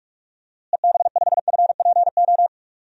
Morse code, heard as a keyed tone, about 44 wpm, 700 Hz. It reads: EB5FPO
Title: DXCC-100 Call Signs - 6 Characters - Repeated 2x - 45wpm